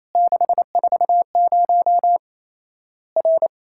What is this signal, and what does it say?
Morse code, heard as a keyed tone, about 28 wpm, 700 Hz. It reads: 640 R
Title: Worked All Europe DX Contest - QTC Exchange - 28wpm